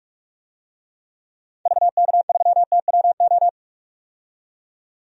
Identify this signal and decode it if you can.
Morse code, heard as a keyed tone, about 43 wpm, 700 Hz. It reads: VK3TWO
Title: DXCC-100 Call Signs - 6 Characters - 45wpm